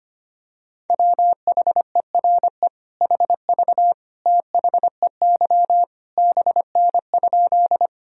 Morse code, heard as a keyed tone, about 25 wpm, 700 Hz. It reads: WHERE HV THEY BN?
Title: Sentences from Top 200 Words and Abbreviations 25wpm